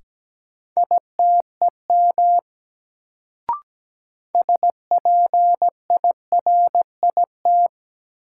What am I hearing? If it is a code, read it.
Morse code, heard as a keyed tone, about 17 wpm, 700 Hz. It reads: ITEM SPIRIT
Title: Sentences from Top 1400 Words - New Word Drill Alt - 17wpm